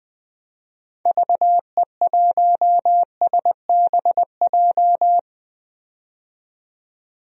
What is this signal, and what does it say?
Morse code, heard as a keyed tone, about 20 wpm, 700 Hz. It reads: VE1SBJ
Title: DXCC-100 Call Signs - 6 Characters - 20wpm